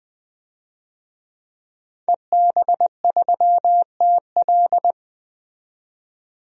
Morse code, heard as a keyed tone, about 21 wpm, 700 Hz. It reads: EB3TL
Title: DXCC-100 Call Signs - 5 Characters - 20wpm